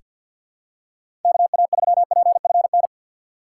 Morse code, heard as a keyed tone, about 46 wpm, 700 Hz. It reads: KA3PFN
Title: USA Call Signs - 2x Character Spacing - 50wpm